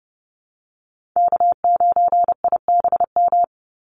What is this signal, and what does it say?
Morse code, heard as a keyed tone, about 30 wpm, 700 Hz. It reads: K9IBM
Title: Call Signs Repeated 2x - 30wpm